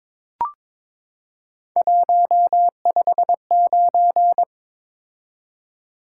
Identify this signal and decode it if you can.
Morse code, heard as a keyed tone, about 22 wpm, 700 Hz. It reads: 159
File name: practice / 22wpm / Three Letter-Number Combinations - 22wpm